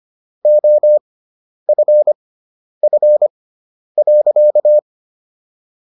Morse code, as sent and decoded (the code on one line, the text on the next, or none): --- ..-. ..-. .-.-.-
OFF.